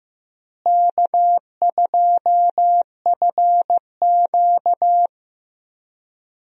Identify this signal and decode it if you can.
Morse code, heard as a keyed tone, about 15 wpm, 700 Hz. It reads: K2FQ
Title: USA Call Signs - 4 Characters - Repeated 3x - 15wpm